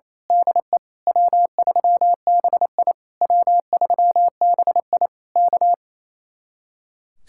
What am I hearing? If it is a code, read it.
Morse code, heard as a keyed tone, about 26 wpm, 700 Hz. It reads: DE W3BI W3BI K
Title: Speed-Racing Common QSO Phrases - 28wpm